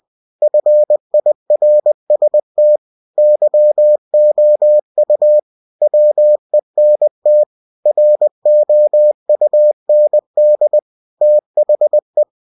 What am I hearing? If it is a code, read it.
Morse code, heard as a keyed tone, about 20 wpm, 600 Hz. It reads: FIRST YOU WENT ROUND THE
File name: cw-books / winnie-the-pooh / winnie-20wpm-page-0025